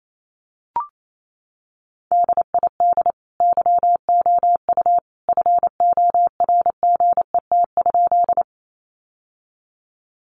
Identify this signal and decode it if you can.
Morse code, heard as a keyed tone, about 28 wpm, 700 Hz. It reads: DID YOU FORGET?
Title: Sentences from Top 1200 Words 28wpm